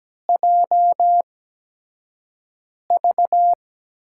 Morse code, heard as a keyed tone, about 17 wpm, spaced as for 5 wpm, 700 Hz. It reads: JV
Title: USA Call Signs - 4 Characters - 8x Character Spacing - 17wpm